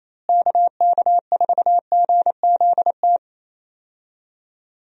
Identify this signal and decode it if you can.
Morse code, heard as a keyed tone, about 28 wpm, 700 Hz. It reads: KK4GZT
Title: USA Call Signs - 6 Characters - Repeated 3x - 28wpm